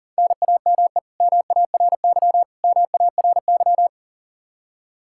Morse code, heard as a keyed tone, about 37 wpm, 700 Hz. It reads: NAME MARY MARY
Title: K1USN SST QSO Practice 40wpm